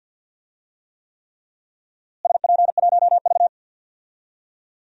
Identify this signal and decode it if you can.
Morse code, heard as a keyed tone, about 51 wpm, 700 Hz. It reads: SP1V
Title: DXCC-100 Call Signs - 4 Characters - Repeated 3x - 50wpm